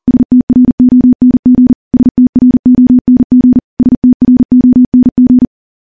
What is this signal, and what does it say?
Morse code, heard as a keyed tone, about 43 wpm, 250 Hz. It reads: STRONG STRONG STRONG
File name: morse / morse-training-40WPM